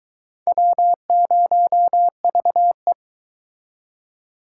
Morse code, heard as a keyed tone, about 21 wpm, 700 Hz. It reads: W0VE